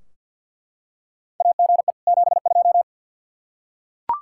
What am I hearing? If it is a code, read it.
Morse code, heard as a keyed tone, about 46 wpm, 700 Hz. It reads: AGE 72
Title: Practice QSO - Exchange 3 - More Info - 50wpm